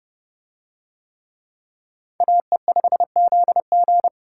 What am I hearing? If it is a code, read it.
Morse code, heard as a keyed tone, about 29 wpm, 700 Hz. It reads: AE5ZG